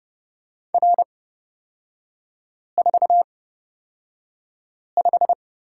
Morse code, heard as a keyed tone, about 30 wpm, spaced as for 5 wpm, 700 Hz. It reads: R45
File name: Lesson 36e - BT - Three Characters 30wpm